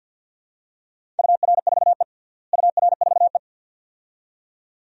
Morse code, heard as a keyed tone, about 51 wpm, 700 Hz. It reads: UR4E UR4E
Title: DXCC-100 Call Signs - 4 Characters - Repeated 2x - 50wpm